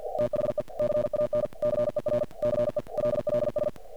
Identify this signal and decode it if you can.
Morse code, heard as a keyed tone, about 47 wpm, 600 Hz, with noise. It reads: THE CANCER CELLS
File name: archived / CRSnet